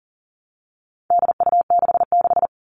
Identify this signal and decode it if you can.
Morse code, heard as a keyed tone, about 40 wpm, 700 Hz. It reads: DU66
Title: All Cut Numbers - 4-Digit 40wpm